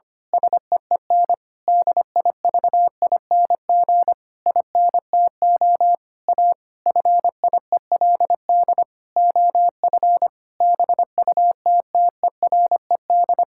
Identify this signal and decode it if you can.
Morse code, heard as a keyed tone, about 25 wpm, 700 Hz. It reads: SEEN DIVING INTO A FIELD OF BUTTERED
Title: Pangram Challenge v2 - 25wpm